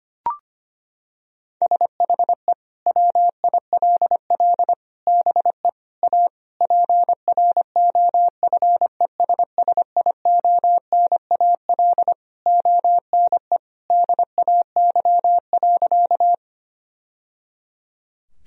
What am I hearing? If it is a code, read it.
Morse code, heard as a keyed tone, about 25 wpm, 700 Hz. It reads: SHE WILL BE A PROFESSIONAL ONE DAY.
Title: Sentences from Top 1000 Words - Review Part 4 - 25wpm